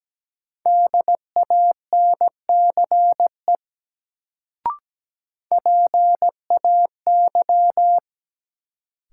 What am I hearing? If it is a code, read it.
Morse code, heard as a keyed tone, about 17 wpm, 700 Hz. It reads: DANCE PAY
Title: Sentences from Top 700 Words - New Word Drill - 17wpm